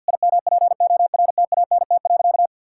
Morse code, heard as a keyed tone, about 50 wpm, 700 Hz. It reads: IMPORTANT.
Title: Sentences from Top 1000 Words - Review Part 6 - 50wpm